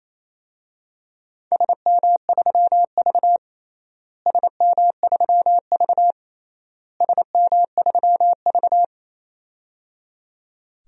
Morse code, heard as a keyed tone, about 28 wpm, 700 Hz. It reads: SM3V SM3V SM3V
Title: DXCC-100 Call Signs - 4 Characters - Repeated 3x - 28wpm